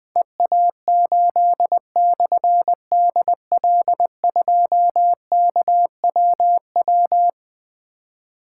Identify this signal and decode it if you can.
Morse code, heard as a keyed tone, about 20 wpm, 700 Hz. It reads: EA8/DL2KWW
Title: Difficult Call Signs 20wpm